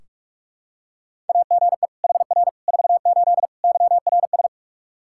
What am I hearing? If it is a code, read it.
Morse code, heard as a keyed tone, about 43 wpm, 700 Hz. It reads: AGE HR 48 YRS
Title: Practice QSO - Exchange 3 - More Info - 45wpm